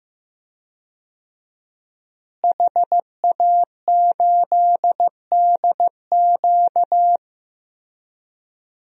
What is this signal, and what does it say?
Morse code, heard as a keyed tone, about 15 wpm, 700 Hz. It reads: HA8DQ
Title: DXCC-100 Call Signs - 5 Characters - Repeated 3x - 15wpm